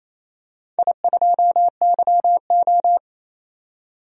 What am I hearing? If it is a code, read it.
Morse code, heard as a keyed tone, about 27 wpm, 700 Hz. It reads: I2YO
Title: DXCC-100 Call Signs - 4 Characters - 28wpm